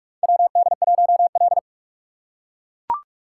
Morse code, heard as a keyed tone, about 42 wpm, 700 Hz. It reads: WD1L
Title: USA Call Signs - 4x Character Spacing - 45wpm